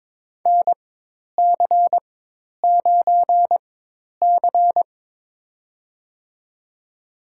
Morse code, heard as a keyed tone, about 21 wpm, spaced as for 10 wpm, 700 Hz. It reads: NC9C
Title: USA Call Signs - 4 Characters - 4x Character Spacing - 22wpm